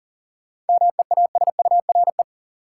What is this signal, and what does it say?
Morse code, heard as a keyed tone, about 40 wpm, 700 Hz. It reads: MEASURE
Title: Sentences from Top 1000 Words - Review Part 1 - New Word Drill - 40wpm